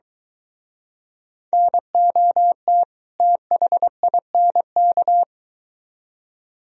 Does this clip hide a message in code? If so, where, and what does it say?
Yes: Morse code from 1.5 to 5.2 s, heard as a keyed tone, about 23 wpm, 700 Hz. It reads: NOT THINK